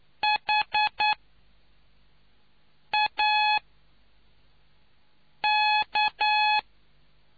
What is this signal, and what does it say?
Morse code, heard as a keyed tone, about 9 wpm, spaced as for 4 wpm, 850 Hz. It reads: HAK